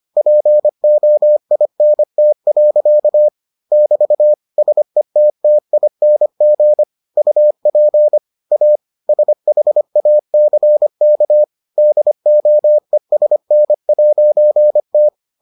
Morse code, heard as a keyed tone, about 25 wpm, 600 Hz. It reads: POINT. <BT> SETTING UP A SHACK DOESN'T